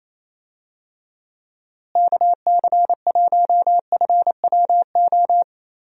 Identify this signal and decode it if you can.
Morse code, heard as a keyed tone, about 28 wpm, 700 Hz. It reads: KC1FWO
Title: USA Call Signs - 6 Characters - Repeated 3x - 28wpm